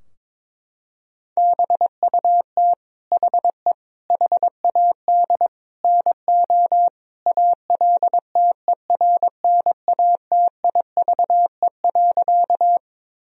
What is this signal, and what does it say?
Morse code, heard as a keyed tone, about 22 wpm, 700 Hz. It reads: BUT HE HAD NO ALTERNATIVE.
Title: Sentences from Top 1000 Words - Review Part 4 - 22wpm